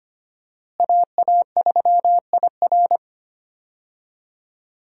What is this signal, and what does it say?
Morse code, heard as a keyed tone, about 25 wpm, 700 Hz. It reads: AA3IR
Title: USA Call Signs - 5 Characters - Repeated 3x - 25wpm